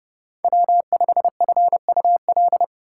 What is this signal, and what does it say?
Morse code, heard as a keyed tone, about 29 wpm, 700 Hz. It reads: W5FUL